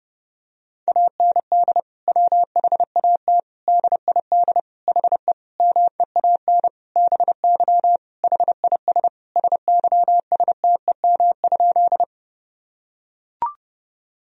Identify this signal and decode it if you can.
Morse code, heard as a keyed tone, about 30 wpm, 700 Hz. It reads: AND WHAT DID HE MEAN BY HIS SYSTEM?
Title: Sentences from Top 1000 Words - Review Part 3 - 30wpm